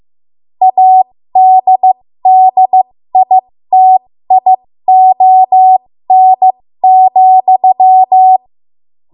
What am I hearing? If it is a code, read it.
Morse code, heard as a keyed tone, about 15 wpm, spaced as for 13 wpm, 750 Hz. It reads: ADDITION,
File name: Morse / 13WPM